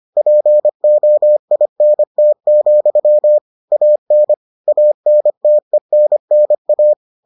Morse code, heard as a keyed tone, about 25 wpm, 600 Hz. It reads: POINT, AN ANTENNA